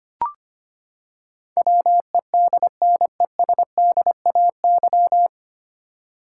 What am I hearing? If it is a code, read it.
Morse code, heard as a keyed tone, about 25 wpm, 700 Hz. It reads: WEDNESDAY
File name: Nine Letter Words - 25wpm